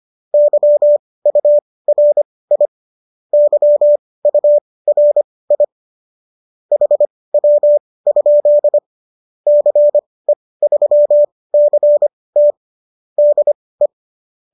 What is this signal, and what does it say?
Morse code, heard as a keyed tone, about 25 wpm, spaced as for 18 wpm, 590 Hz. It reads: YURI YURI HW? CE3CT DE